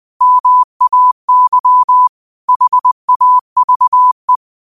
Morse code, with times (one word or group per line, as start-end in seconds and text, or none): 0.2-2.1 MAY
2.5-4.4 HAVE